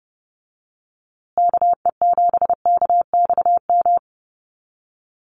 Morse code, heard as a keyed tone, about 30 wpm, 700 Hz. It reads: KE7KXM
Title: USA Call Signs Repeated 2x 30wpm